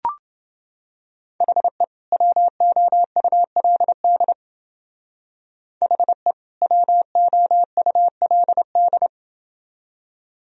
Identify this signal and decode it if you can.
Morse code, heard as a keyed tone, about 28 wpm, 700 Hz. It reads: HE WOULD HE WOULD